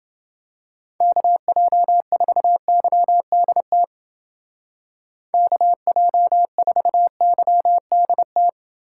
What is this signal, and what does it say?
Morse code, heard as a keyed tone, about 29 wpm, 700 Hz. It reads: KJ4YDT KJ4YDT